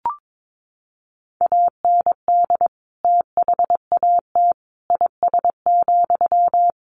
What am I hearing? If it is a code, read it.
Morse code, heard as a keyed tone, about 22 wpm, 700 Hz. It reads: AND THAT IS,
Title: Sentences from Top 1500 Words 22wpm